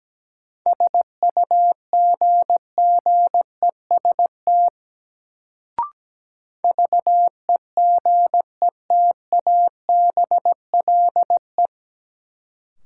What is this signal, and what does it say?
Morse code, heard as a keyed tone, about 17 wpm, 700 Hz. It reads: SUGGEST VEGETABLE
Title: Sentences from Top 1300 Words - New Word Drill - 17wpm